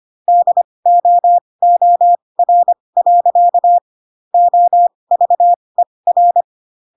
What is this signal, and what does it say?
Morse code, heard as a keyed tone, about 25 wpm, spaced as for 20 wpm, 700 Hz. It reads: DOOR. OVER